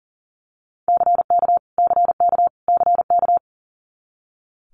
Morse code, heard as a keyed tone, about 40 wpm, 700 Hz. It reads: CK CK CK
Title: Multi-letter Phonemes Repeated 3x 40wpm